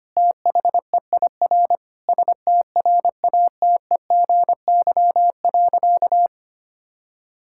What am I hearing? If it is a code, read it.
Morse code, heard as a keyed tone, about 25 wpm, 700 Hz. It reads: THEIR STRATEGY.
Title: Sentences from Top 1000 Words - Review Part 5 - 25wpm